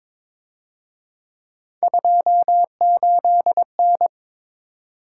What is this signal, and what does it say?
Morse code, heard as a keyed tone, about 26 wpm, 700 Hz. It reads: 28N
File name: Speed-Racing - Three Letter-Number Combinations - 22wpm